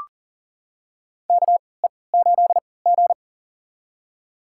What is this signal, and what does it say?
Morse code, heard as a keyed tone, about 40 wpm, spaced as for 23 wpm, 700 Hz. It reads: KE8G
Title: USA Call Signs - 4 Characters - 3x Character Spacing - 40wpm